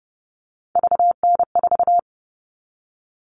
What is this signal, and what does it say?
Morse code, heard as a keyed tone, about 30 wpm, 700 Hz. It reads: VN4
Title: All Cut Numbers - 3-Digit 30wpm